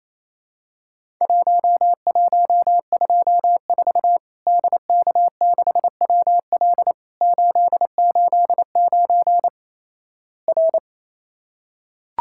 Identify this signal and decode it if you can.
Morse code, heard as a keyed tone, about 28 wpm, 700 Hz. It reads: 1124 DK6WL 889 R
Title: Worked All Europe DX Contest - QTC Exchange - 28wpm